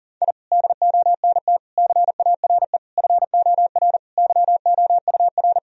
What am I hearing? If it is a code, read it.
Morse code, heard as a keyed tone, about 37 wpm, 700 Hz. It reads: I DONT CARE FOR YOUR
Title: Sentences from Top 1000 Words - Review Part 3 - 40wpm